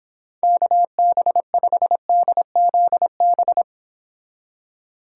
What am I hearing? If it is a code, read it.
Morse code, heard as a keyed tone, about 26 wpm, 700 Hz. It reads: KB5DZB